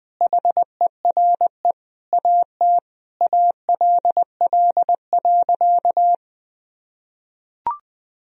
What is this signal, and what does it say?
Morse code, heard as a keyed tone, about 20 wpm, 700 Hz. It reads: HERE AT ALL.
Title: Sentences from Top 1000 Words - Review Part 1 - 20wpm